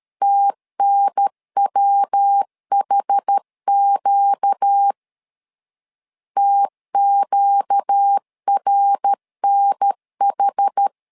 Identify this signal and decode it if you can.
Morse code, heard as a keyed tone, about 13 wpm, 800 Hz. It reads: TNWHQ TQRNH
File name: LEZIONI / 4 Q 12wpm 1833